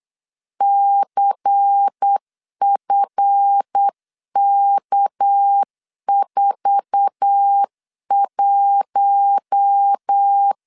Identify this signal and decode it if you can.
Morse code, heard as a keyed tone, about 8 wpm, 800 Hz. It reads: CFK41
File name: CW / DET / 1 9 1500C S A 8wpm 5637